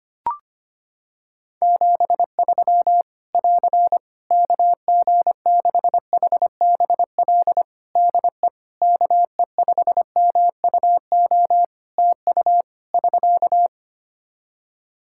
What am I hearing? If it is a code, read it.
Morse code, heard as a keyed tone, about 25 wpm, 700 Hz. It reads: 73 <AR> KG6HBL DE KE5MUO TU <SK>
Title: Practice QSO - Exchange 4 - Ending - 25wpm